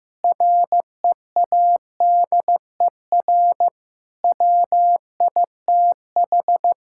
Morse code, heard as a keyed tone, about 15 wpm, 700 Hz. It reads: READER WITH
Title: Pangram Challenge v2 - 15wpm